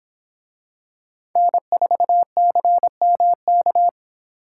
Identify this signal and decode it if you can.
Morse code, heard as a keyed tone, about 25 wpm, 700 Hz. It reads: N4CMK